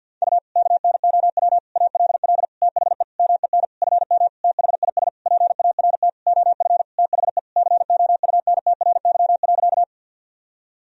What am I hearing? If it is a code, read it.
Morse code, heard as a keyed tone, about 47 wpm, 700 Hz. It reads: U KNOW ALL THE MEN FM THIS PART OF THE COUNTRY.
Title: Sentences from Top 300 Words and Abbreviations 50wpm